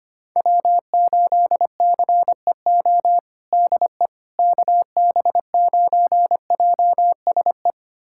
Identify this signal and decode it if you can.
Morse code, heard as a keyed tone, about 25 wpm, 700 Hz. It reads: W8CEO DE KB9JSE
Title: USA Callsign DE Callsign 25wpm